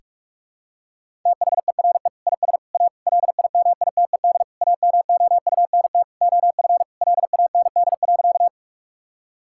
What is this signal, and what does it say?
Morse code, heard as a keyed tone, about 41 wpm, 700 Hz. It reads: THERE IS A LIMITED AMOUNT OF LAND.
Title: Sentences from Top 1000 Words - Review Part 6 - 45wpm